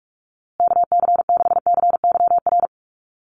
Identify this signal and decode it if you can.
Morse code, heard as a keyed tone, about 44 wpm, 700 Hz. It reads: KC6CYR
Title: USA Call Signs Repeated 2x 45wpm